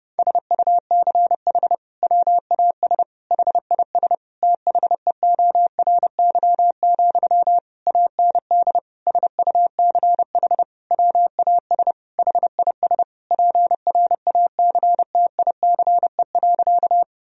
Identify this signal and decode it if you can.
Morse code, heard as a keyed tone, about 31 wpm, 700 Hz. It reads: SUCH WAS HIS THEORY, AND SUCH WAS HIS PRACTICE.
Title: Sentences from Top 1000 Words - Review Part 4 - 30wpm